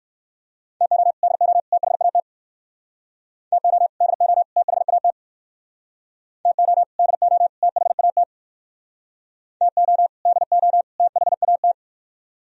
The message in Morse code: TO DO THAT TO DO THAT TO DO THAT TO DO THAT

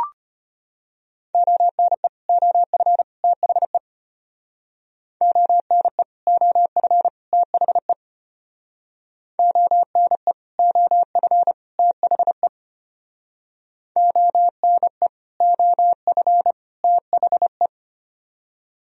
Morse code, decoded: ONE OF THE ONE OF THE ONE OF THE ONE OF THE

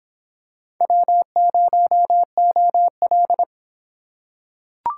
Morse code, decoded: W0OL

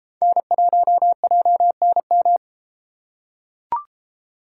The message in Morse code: N1JNM

